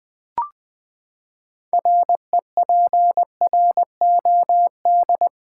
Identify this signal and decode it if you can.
Morse code, heard as a keyed tone, about 20 wpm, 700 Hz. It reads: REPROD
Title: Build A Word - 6 Letters At A Time - 12-Letter Words - 20wpm